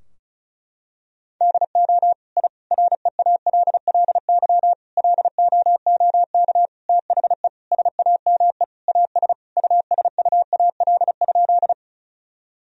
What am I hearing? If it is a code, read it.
Morse code, heard as a keyed tone, about 36 wpm, 700 Hz. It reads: DO I REALLY LOOK THE SAME AS USUAL?
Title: Sentences from Top 1000 Words - Review Part 2 - 35wpm